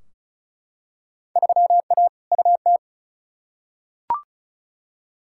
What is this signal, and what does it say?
Morse code, heard as a keyed tone, about 34 wpm, 700 Hz. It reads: 3A UT
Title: Field Day Exchanges - 35wpm